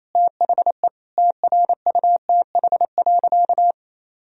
Morse code, heard as a keyed tone, about 28 wpm, 700 Hz. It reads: THE TRUTH.
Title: Sentences from Top 1000 Words - Review Part 2 - 28wpm